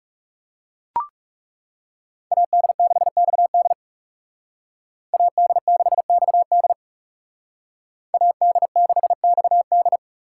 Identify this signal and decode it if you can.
Morse code, heard as a keyed tone, about 37 wpm, 700 Hz. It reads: AD6XD AD6XD AD6XD
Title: Speed Racing - USA 2x2 Call Signs 30wpm